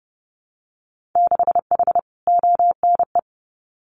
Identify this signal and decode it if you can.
Morse code, heard as a keyed tone, about 30 wpm, 700 Hz. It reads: 6H ONE
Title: Winter Field Day Exchanges Encore 30wpm